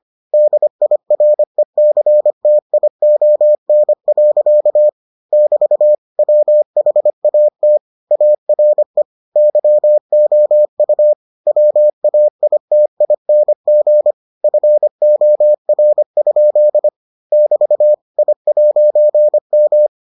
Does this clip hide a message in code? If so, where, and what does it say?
Yes: Morse code from 0.3 to 19.9 s, heard as a keyed tone, about 25 wpm, 600 Hz. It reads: DIRECTION. <BT> WHAT ARE YOU WAITING FOR? <BT> I'M